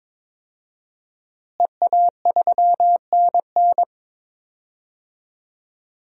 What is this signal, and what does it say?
Morse code, heard as a keyed tone, about 22 wpm, 700 Hz. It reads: EA3NN
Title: DXCC-100 Call Signs - 5 Characters - Repeated 2x - 22wpm